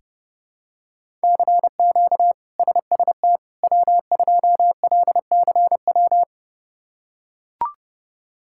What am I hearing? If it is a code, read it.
Morse code, heard as a keyed tone, about 32 wpm, 700 Hz. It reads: CQ SST W2LCW
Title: K1USN SST QSO Practice - 30wpm